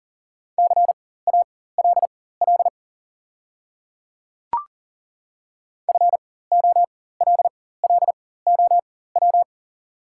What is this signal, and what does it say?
Morse code, heard as a keyed tone, about 40 wpm, spaced as for 19 wpm, 690 Hz. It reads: CALL FOLLOW
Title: 4x Character Spacing - Top 300 Words - 40wpm